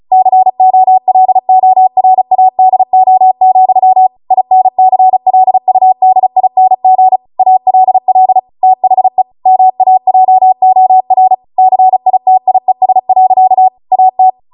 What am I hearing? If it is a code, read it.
Morse code, heard as a keyed tone, about 36 wpm, 750 Hz. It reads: COLORADO, INCLUDING ALL THE MAJOR CITIES. AT
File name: Morse / 35WPM